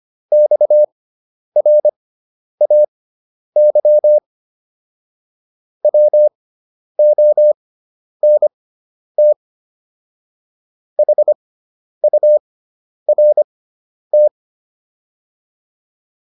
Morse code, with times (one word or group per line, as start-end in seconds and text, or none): 0.3-4.2 XRAY
5.8-9.3 WONT
11.0-14.3 HURT